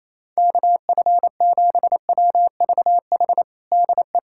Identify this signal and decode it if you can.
Morse code, heard as a keyed tone, about 28 wpm, 700 Hz. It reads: KF7WVH DE
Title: USA Callsign DE Callsign - 6 Characters - 28wpm